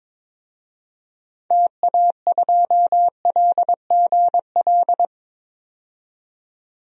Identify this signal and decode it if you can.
Morse code, heard as a keyed tone, about 22 wpm, 700 Hz. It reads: TA2LGL
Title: DXCC-100 Call Signs - 6 Characters - 22wpm